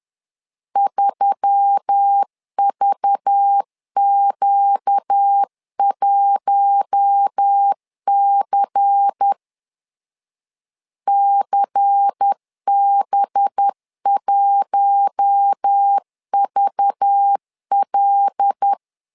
Morse code, heard as a keyed tone, about 10 wpm, 800 Hz. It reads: 3VQ1C CB1VL
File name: LEZIONI / 5 LBVC / 5 LBVC 10wpm 2349